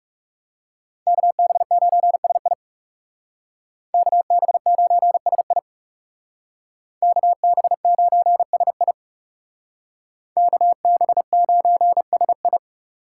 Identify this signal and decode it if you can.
Morse code, heard as a keyed tone, about 38 wpm, 700 Hz. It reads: KB9SI KB9SI KB9SI KB9SI